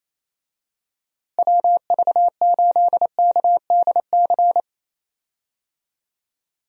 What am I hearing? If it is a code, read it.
Morse code, heard as a keyed tone, about 28 wpm, 700 Hz. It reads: WV8KDC
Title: USA Call Signs - 6 Characters - Repeated 2x - 28wpm